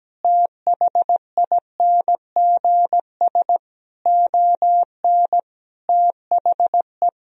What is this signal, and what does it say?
Morse code, heard as a keyed tone, about 17 wpm, 700 Hz. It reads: THINGS ON THE
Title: Sentences from Top 1000 Words - Review Part 4 - 17wpm